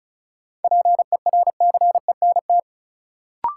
Morse code, heard as a keyed tone, about 36 wpm, 700 Hz. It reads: PERCENT
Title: Sets of 2 Words - New Word Drill - 35wpm